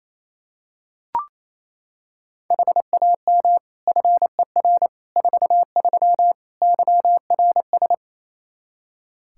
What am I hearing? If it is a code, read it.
Morse code, heard as a keyed tone, about 28 wpm, 700 Hz. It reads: HAM FER 43 YRS
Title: K1USN SST QSO Practice 28wpm